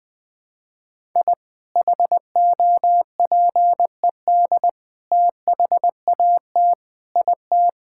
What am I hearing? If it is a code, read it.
Morse code, heard as a keyed tone, about 20 wpm, 700 Hz. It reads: I HOPED THAT IT
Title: Sentences with 4-Word Phrases - Spoken Beginning - 20wpm